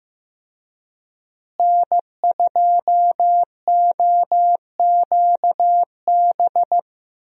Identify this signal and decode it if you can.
Morse code, heard as a keyed tone, about 15 wpm, 700 Hz. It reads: N2OQB
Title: USA Call Signs - 5 Characters - Repeated 3x - 15wpm